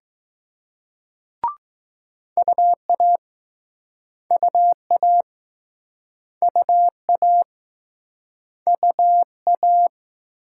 Morse code, heard as a keyed tone, about 21 wpm, 700 Hz. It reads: UA UA UA UA